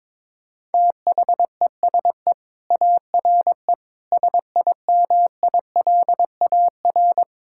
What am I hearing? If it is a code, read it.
Morse code, heard as a keyed tone, about 21 wpm, 700 Hz. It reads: THESE ARE SIMILAR
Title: Sentences from Top 1000 Words - Review Part 5 - 22wpm